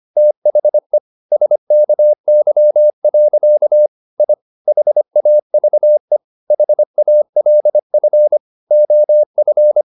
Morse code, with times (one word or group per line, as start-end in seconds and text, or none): 0.2-1.0 THE
1.3-3.9 SKY.
4.2-4.3 I
4.7-6.2 HAVE
6.5-8.4 HALF
8.7-9.8 OF